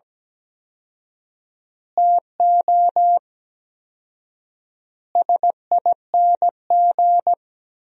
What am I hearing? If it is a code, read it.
Morse code, heard as a keyed tone, about 17 wpm, 700 Hz. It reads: TO SING
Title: 4x Word-Spacing - Sentences from Top 400 Words - 17wpm